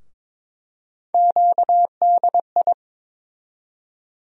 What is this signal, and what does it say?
Morse code, heard as a keyed tone, about 22 wpm, 700 Hz. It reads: QDI
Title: USA Call Sign Postfixes - 3 Letters 22wpm